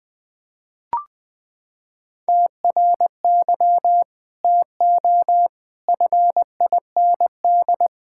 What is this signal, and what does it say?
Morse code, heard as a keyed tone, about 21 wpm, 700 Hz. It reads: TRY TO FIND
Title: Sentences with 2-Word Phrases - 20wpm